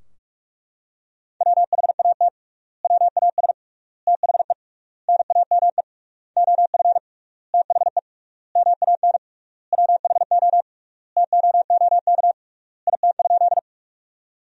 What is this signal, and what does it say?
Morse code, heard as a keyed tone, about 43 wpm, 700 Hz. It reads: WHAT WAS THE NAME OF THE MAN WHO TOOK IT?
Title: 3x Word-Spacing - Sentences from Top 300 Words - 45wpm